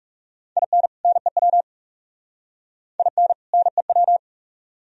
Morse code, heard as a keyed tone, about 40 wpm, 700 Hz. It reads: IN NEW IN NEW